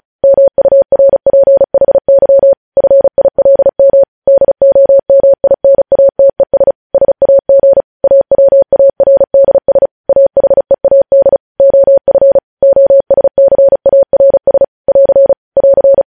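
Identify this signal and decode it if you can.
Morse code, heard as a keyed tone, about 36 wpm, 550 Hz. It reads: MURPHY FILM DOMINATES SAG AWARDS AHEAD OF OSCARS <AR> <AR>
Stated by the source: Morse code transmission of news headlines, preceded by vvv.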